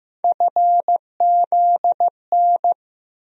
Morse code, heard as a keyed tone, about 15 wpm, 700 Hz. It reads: FZN
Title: USA Call Sign Postfixes - 3 Letters 15wpm